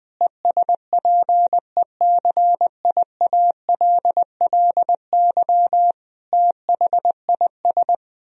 Morse code, heard as a keyed tone, about 21 wpm, 700 Hz. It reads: ESPECIALLY THIS
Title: Sentences from Top 1000 Words - Review Part 4 - 20wpm